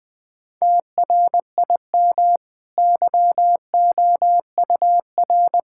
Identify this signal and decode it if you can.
Morse code, heard as a keyed tone, about 20 wpm, 700 Hz. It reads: TRIM YOUR
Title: Easy Sentences with Hints - 20wpm